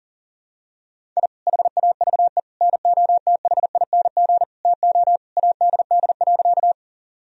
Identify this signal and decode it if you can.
Morse code, heard as a keyed tone, about 40 wpm, 700 Hz. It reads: I HAVE NOTHING TO ADD.
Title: Sentences from Top 1000 Words - Review Part 1 - 40wpm